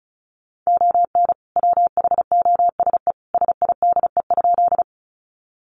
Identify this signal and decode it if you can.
Morse code, heard as a keyed tone, about 34 wpm, 700 Hz. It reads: ON WHOSE SIDE?
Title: Sentences from Top 1200 Words 35wpm